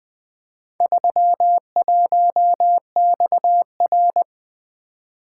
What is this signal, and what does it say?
Morse code, heard as a keyed tone, about 24 wpm, 700 Hz. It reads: 31XR
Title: Speed-Racing - Four Letter-Number Combinations - 20wpm